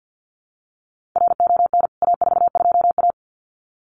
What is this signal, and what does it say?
Morse code, heard as a keyed tone, about 50 wpm, 700 Hz. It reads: RON A42A
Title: CWT Exchange 50wpm